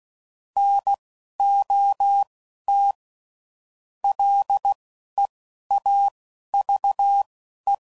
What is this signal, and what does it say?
Morse code, heard as a keyed tone, about 16 wpm, spaced as for 11 wpm, 800 Hz. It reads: NOT LEAVE